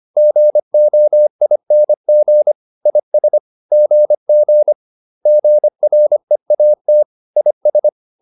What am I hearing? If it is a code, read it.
Morse code, heard as a keyed tone, about 25 wpm, 600 Hz. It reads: GOING IS GG GREAT IS